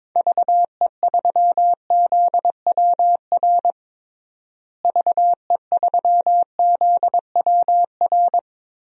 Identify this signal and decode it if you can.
Morse code, heard as a keyed tone, about 22 wpm, 700 Hz. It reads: VE3ZWR VE3ZWR
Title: DXCC-100 Call Signs - 6 Characters - Repeated 3x - 22wpm